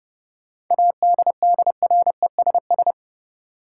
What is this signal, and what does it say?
Morse code, heard as a keyed tone, about 30 wpm, 700 Hz. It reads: ADDRESS
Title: Sets of 5 Words Encore - New Word Drill - 30wpm